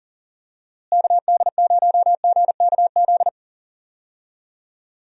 Morse code, heard as a keyed tone, about 40 wpm, 700 Hz. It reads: KD0GKZ
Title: USA Call Signs - 6 Characters - Repeated 3x - 40wpm